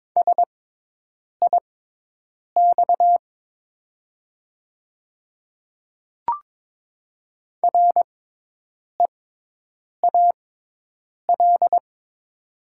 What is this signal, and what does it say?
Morse code, heard as a keyed tone, about 22 wpm, spaced as for 8 wpm, 700 Hz. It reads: SIX REAL
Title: 6x Character Spacing - Top 500 Words - 22wpm